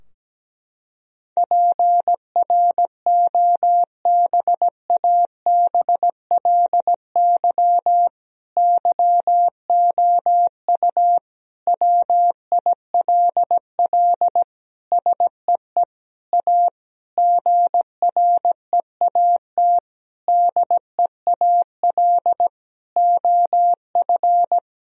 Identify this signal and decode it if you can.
Morse code, heard as a keyed tone, about 17 wpm, 700 Hz. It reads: PROBABLY YOU WILL SEE A GREAT DEAL OF
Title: Sentences from Top 1000 Words - Review Part 4 - 17wpm